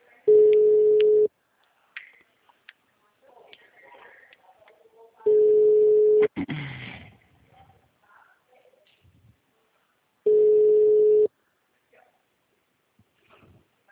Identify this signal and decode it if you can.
Ringback tone (Europe, 425 Hz).